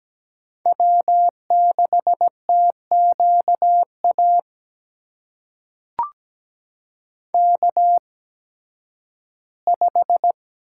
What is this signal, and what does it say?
Morse code, heard as a keyed tone, about 17 wpm, spaced as for 11 wpm, 700 Hz. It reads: W6TQA K5
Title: USA Call Signs - 5 Characters - 8x Character Spacing - 17wpm